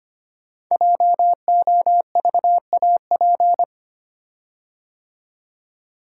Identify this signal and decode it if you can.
Morse code, heard as a keyed tone, about 25 wpm, 700 Hz. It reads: JOVAP
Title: Code Groups - 5 Letters - 25wpm